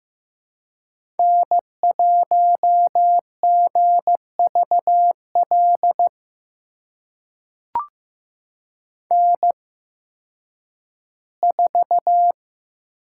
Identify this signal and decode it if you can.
Morse code, heard as a keyed tone, about 15 wpm, spaced as for 10 wpm, 700 Hz. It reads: N1GVL N4
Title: USA Call Signs - 5 Characters - 8x Character Spacing - 15wpm